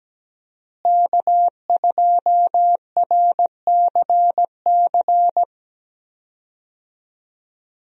Morse code, heard as a keyed tone, about 17 wpm, 700 Hz. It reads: K2RCC
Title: USA Call Signs - 5 Characters - Repeated 2x - 17wpm